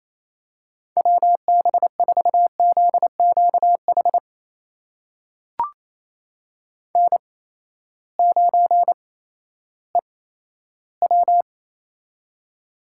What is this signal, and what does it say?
Morse code, heard as a keyed tone, about 28 wpm, spaced as for 14 wpm, 700 Hz. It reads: WB4ZQH N9EW
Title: USA Call Signs - 8x Character Spacing - 28wpm